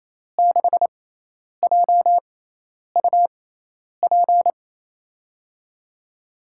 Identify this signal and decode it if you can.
Morse code, heard as a keyed tone, about 28 wpm, spaced as for 10 wpm, 700 Hz. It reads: 6JUP
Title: USA Call Signs - 6 Characters - 6x Character Spacing - 28wpm